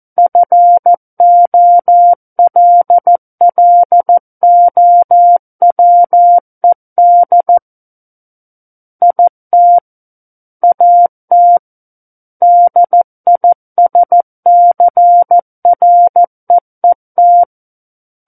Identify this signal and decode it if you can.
Morse code, heard as a keyed tone, about 14 wpm, 700 Hz. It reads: FOLLOWED IT AT DISCREET
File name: Book One / 05-Chapter 5 - 14 WPM